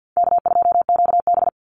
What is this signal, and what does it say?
Morse code, heard as a keyed tone, about 48 wpm, 700 Hz. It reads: K2QB
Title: Easy Call Signs - 50wpm